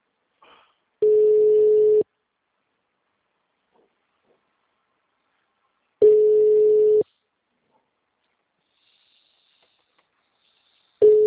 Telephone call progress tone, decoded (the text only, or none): ringback tone